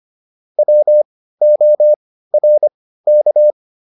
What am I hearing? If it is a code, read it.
Morse code, heard as a keyed tone, about 25 wpm, spaced as for 15 wpm, 600 Hz. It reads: WORK